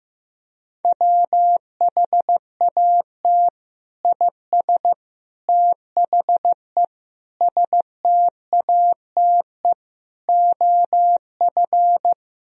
Morse code, heard as a keyed tone, about 15 wpm, 700 Hz. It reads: WHAT IS THE STATE OF
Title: Sentences from Top 1000 Words - Review Part 4 - 15wpm